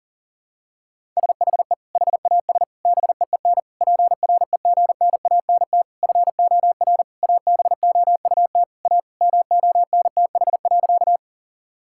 Morse code, heard as a keyed tone, about 37 wpm, 700 Hz. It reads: SHE HAS BEEN PREGNANT FOR ABOUT A MONTH.
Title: Sentences from Top 1000 Words - Review Part 6 - 40wpm